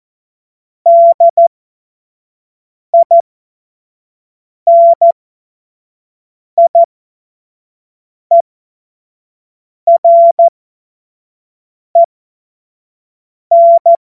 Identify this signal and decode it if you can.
Morse code, heard as a keyed tone, about 14 wpm, spaced as for 5 wpm, 680 Hz. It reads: DINIEREN
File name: Morsekurs / Witze / Witz 11. Januar 2022, 25 F70